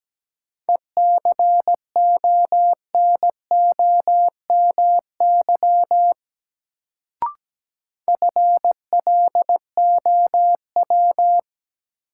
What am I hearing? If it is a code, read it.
Morse code, heard as a keyed tone, about 17 wpm, 700 Hz. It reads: ECONOMY FLOW
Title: Sentences from Top 1200 Words - New Word Drill Alt - 17wpm